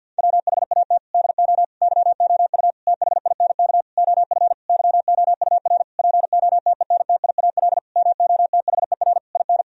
Morse code, heard as a keyed tone, about 50 wpm, 700 Hz. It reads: WHAT DO YOU THINK OF YOUR POTENTIAL MOTHER IN
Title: Sentences from Top 1000 Words - Review Part 4 - 50wpm